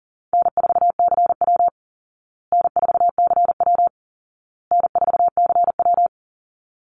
Morse code, heard as a keyed tone, about 40 wpm, 700 Hz. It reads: N4CW N4CW N4CW
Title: Easy Call Signs Repeated 3x - 40wpm